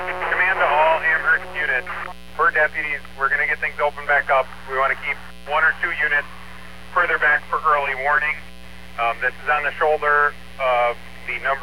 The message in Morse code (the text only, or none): none